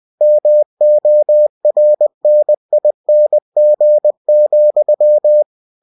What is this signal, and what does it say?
Morse code, heard as a keyed tone, about 20 wpm, 600 Hz. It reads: MORNING,
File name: cw-books / winnie-the-pooh / winnie-20wpm-page-0041